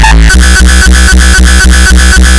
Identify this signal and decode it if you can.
There is no code.